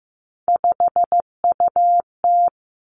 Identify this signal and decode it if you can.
Morse code, heard as a keyed tone, about 15 wpm, 700 Hz. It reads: 5UT
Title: Three Challenging Letter-Number Combinations 15wpm